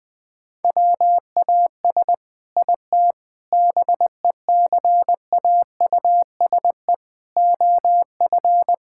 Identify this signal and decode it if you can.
Morse code, heard as a keyed tone, about 20 wpm, 700 Hz. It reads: WAS IT BECAUSE OF
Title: Sentences from Top 1000 Words - Review Part 6 - 20wpm